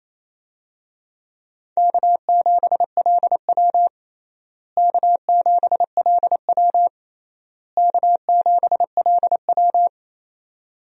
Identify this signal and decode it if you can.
Morse code, heard as a keyed tone, about 28 wpm, 700 Hz. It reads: K7LW K7LW K7LW
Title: USA Call Signs - 4 Characters - Repeated 3x - 28wpm